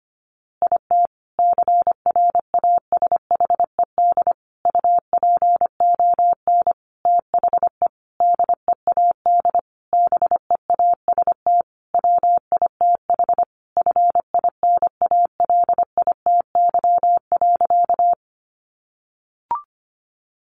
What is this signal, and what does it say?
Morse code, heard as a keyed tone, about 25 wpm, 700 Hz. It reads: IT CRASHED UPON THE DEAD BEAST WITH FINALITY.
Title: A Sound Of Thunder 25wpm